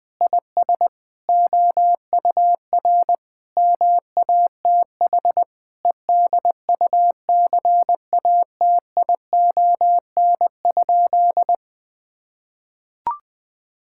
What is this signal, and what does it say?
Morse code, heard as a keyed tone, about 20 wpm, 700 Hz. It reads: IS OUR MATH EDUCATION?
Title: Sentences from Top 1000 Words - Review Part 6 - 20wpm